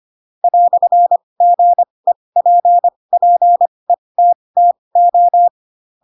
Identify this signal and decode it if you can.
Morse code, heard as a keyed tone, about 25 wpm, spaced as for 20 wpm, 700 Hz. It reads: "GEPPETTO